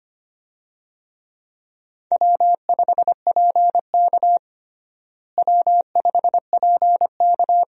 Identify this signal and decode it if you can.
Morse code, heard as a keyed tone, about 25 wpm, 700 Hz. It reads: W5PK W5PK
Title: USA Call Signs - 4 Characters - Repeated 2x - 25wpm